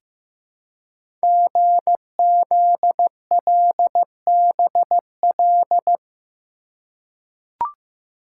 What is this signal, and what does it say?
Morse code, heard as a keyed tone, about 15 wpm, 700 Hz. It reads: GZLBL
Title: Code Groups - 5 Letters - 15wpm